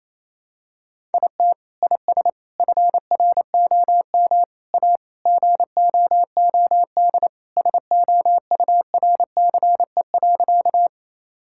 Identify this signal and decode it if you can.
Morse code, heard as a keyed tone, about 28 wpm, 700 Hz. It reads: IT IS FROM A GOOD SOURCE.
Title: Sentences from Top 1000 Words - Review Part 5 - 28wpm